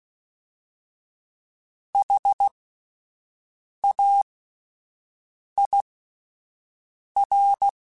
Morse code, heard as a keyed tone, about 16 wpm, spaced as for 5 wpm, 800 Hz. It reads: HAIR
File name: Alphabetical / Alphabetical mp3s / Hs @5